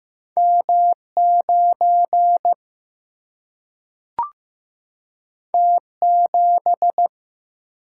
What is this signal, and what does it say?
Morse code, heard as a keyed tone, about 15 wpm, 700 Hz. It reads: M 9 T 7
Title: Single-Letter Single-Number 15wpm